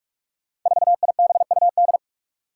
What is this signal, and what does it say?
Morse code, heard as a keyed tone, about 44 wpm, 700 Hz. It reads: 4IBUD
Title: Five Challenging Letter-Number Combinations 45wpm